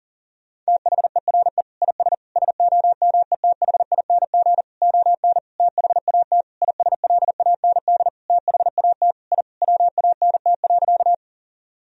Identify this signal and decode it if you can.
Morse code, heard as a keyed tone, about 40 wpm, 700 Hz. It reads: THERE IS SOMETHING ON THAT ISLAND THAT I WANT.
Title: Sentences from Top 1000 Words - Review Part 3 - 40wpm